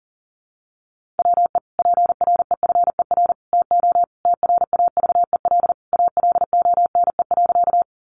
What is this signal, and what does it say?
Morse code, heard as a keyed tone, about 40 wpm, 700 Hz. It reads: WE PREFER TO TRAVEL ALONE.
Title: Sentences from Top 1100 Words 40wpm